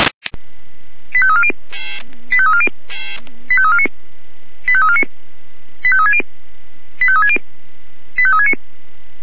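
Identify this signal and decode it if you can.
Five-tone selective call (ZVEI), heard as quick runs of five tones, tones of about 70 ms. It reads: 85340 85340 85349 85349 85360 85360 85369
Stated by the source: ZVEI five-tone selective call received on SDR, code 85340.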